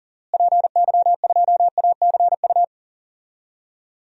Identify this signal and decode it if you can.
Morse code, heard as a keyed tone, about 40 wpm, 700 Hz. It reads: PY2ACU
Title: DXCC-100 Call Signs - 6 Characters - Repeated 3x - 40wpm